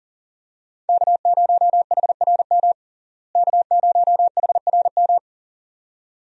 Morse code, heard as a keyed tone, about 40 wpm, 700 Hz. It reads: K0HRM K0HRM
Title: USA Call Signs - 5 Characters - Repeated 2x - 40wpm